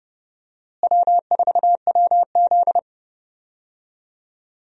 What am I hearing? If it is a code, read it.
Morse code, heard as a keyed tone, about 30 wpm, 700 Hz. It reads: W4WZ
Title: USA Call Signs - 4 Characters - Repeated 3x - 30wpm